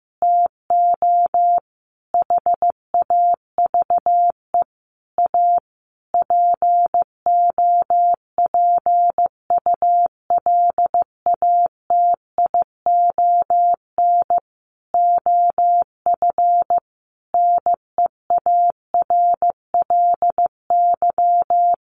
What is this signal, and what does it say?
Morse code, heard as a keyed tone, about 15 wpm, 700 Hz. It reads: TO HAVE A POPULATION OF NEARLY
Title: Sentences from Top 1300 Words 15wpm